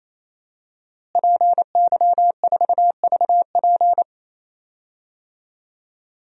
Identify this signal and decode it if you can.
Morse code, heard as a keyed tone, about 27 wpm, 700 Hz. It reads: PY4VP
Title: DXCC-100 Call Signs - 5 Characters - Repeated 3x - 28wpm